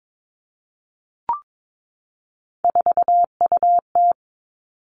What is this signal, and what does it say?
Morse code, heard as a keyed tone, about 22 wpm, 700 Hz. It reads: 4UT
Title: Three Challenging Letter-Number Combinations 22wpm